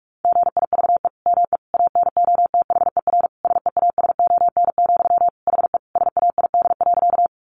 Morse code, heard as a keyed tone, about 44 wpm, 700 Hz. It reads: GIVE ME ANOTHER SEASON, HE SAID.
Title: Sentences from Top 1400 Words 45wpm